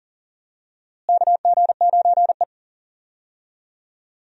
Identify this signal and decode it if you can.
Morse code, heard as a keyed tone, about 37 wpm, 700 Hz. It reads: KG9E
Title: USA Call Signs - 4 Characters - Repeated 2x - 40wpm